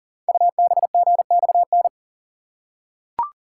Morse code, heard as a keyed tone, about 38 wpm, 700 Hz. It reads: UBGXN
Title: Code Groups - 5 Letters - 40wpm